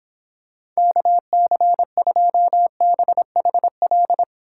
Morse code, heard as a keyed tone, about 26 wpm, 700 Hz. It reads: KC2BHL